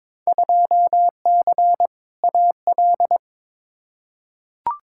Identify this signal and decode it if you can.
Morse code, heard as a keyed tone, about 21 wpm, 700 Hz. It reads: 2C AL
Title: Field Day Exchanges Encore - 22wpm